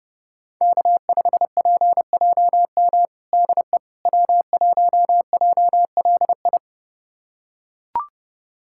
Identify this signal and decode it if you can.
Morse code, heard as a keyed tone, about 32 wpm, 700 Hz. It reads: K5PJM DE W1JLI
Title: USA Callsign DE Callsign - 5 Characters - 30wpm